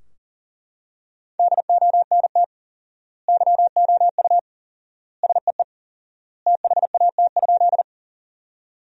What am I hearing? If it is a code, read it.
Morse code, heard as a keyed tone, about 40 wpm, 700 Hz. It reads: DONT YOU SEE THAT?
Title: 4x Word-Spacing - Sentences from Top 300 Words - 40wpm